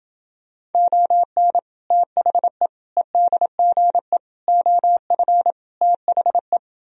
Morse code, heard as a keyed tone, about 30 wpm, 700 Hz. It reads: ON THE EDGE OF THE